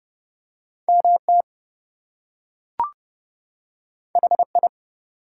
Morse code, heard as a keyed tone, about 31 wpm, 700 Hz. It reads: MT HI
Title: US States and Canadian Provinces 30wpm